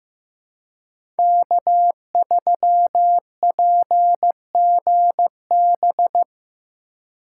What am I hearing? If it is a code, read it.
Morse code, heard as a keyed tone, about 15 wpm, 700 Hz. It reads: K3PGB
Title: USA Call Signs - 2x Character Spacing - 15wpm